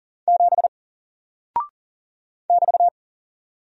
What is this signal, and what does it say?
Morse code, heard as a keyed tone, about 40 wpm, 700 Hz. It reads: 7 <BT>
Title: Single Character (Letter:Number:Punctuation:Prosign) Alt 40wpm